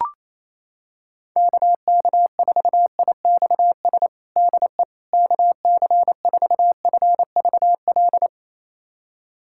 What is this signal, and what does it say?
Morse code, heard as a keyed tone, about 28 wpm, 700 Hz. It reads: KK4IXS DE KC4FVL
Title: USA Callsign DE Callsign - 6 Characters - 28wpm